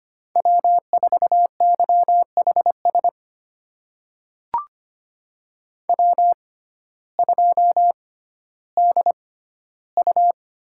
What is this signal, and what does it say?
Morse code, heard as a keyed tone, about 25 wpm, spaced as for 14 wpm, 700 Hz. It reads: W4YHS W2DU
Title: USA Call Signs - 5 Characters - 6x Character Spacing - 25wpm